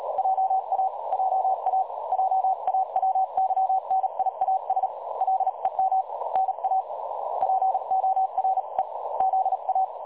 Morse code, heard as a keyed tone, about 36 wpm, spaced as for 30 wpm, 760 Hz. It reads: JA JA JA WYNIASLEM NA POREZA
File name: dlazaawansowanych / nagraniazpasma / websdr recording 3565.0kHz